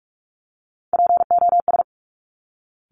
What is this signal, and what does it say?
Morse code, heard as a keyed tone, about 46 wpm, 700 Hz. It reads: POS
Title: Top Three Letter Combinations in Words 45wpm